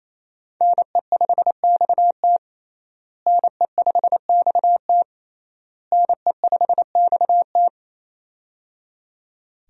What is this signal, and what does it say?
Morse code, heard as a keyed tone, about 28 wpm, 700 Hz. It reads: NE5XT NE5XT NE5XT
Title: USA Call Signs - 5 Characters - Repeated 3x - 28wpm